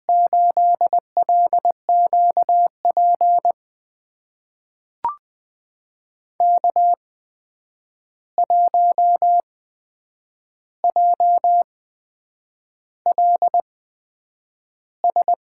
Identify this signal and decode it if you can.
Morse code, heard as a keyed tone, about 20 wpm, spaced as for 8 wpm, 700 Hz. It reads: 8LQP K1JLS
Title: USA Call Signs - 5 Characters - 8x Character Spacing - 20wpm